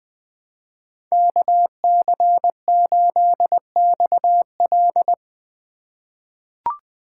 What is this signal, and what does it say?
Morse code, heard as a keyed tone, about 20 wpm, 700 Hz. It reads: KC8XL
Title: Code Groups - 5 Letter:Numbers - 20wpm